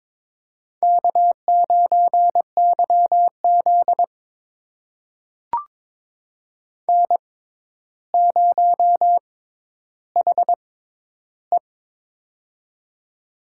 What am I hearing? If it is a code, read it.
Morse code, heard as a keyed tone, about 22 wpm, spaced as for 11 wpm, 700 Hz. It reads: K9YZ N0HE
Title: USA Call Signs - 4 Characters - 6x Character Spacing - 22wpm